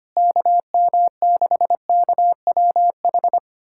Morse code, heard as a keyed tone, about 25 wpm, 700 Hz. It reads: KM6KWH
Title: USA Callsign DE Callsign 25wpm